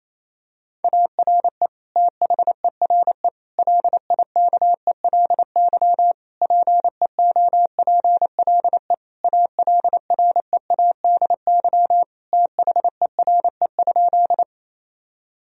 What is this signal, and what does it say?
Morse code, heard as a keyed tone, about 28 wpm, 700 Hz. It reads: ARE THERE LIKELY PEOPLE ALREADY THERE?
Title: Sentences from Top 1000 Words - Review Part 4 - 28wpm